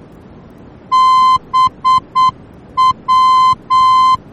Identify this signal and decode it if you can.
Morse code, heard as a keyed tone, about 8 wpm, 1040 Hz. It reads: BW
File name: germany / mx / BW-D